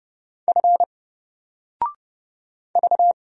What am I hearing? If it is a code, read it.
Morse code, heard as a keyed tone, about 31 wpm, 700 Hz. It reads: F V
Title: Single Character (Letter:Number:Punctuation:Prosign) 30wpm